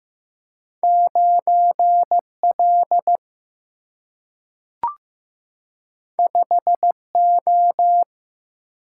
Morse code, heard as a keyed tone, about 15 wpm, 700 Hz. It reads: 9L 5O
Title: Single-Letter Single-Number 15wpm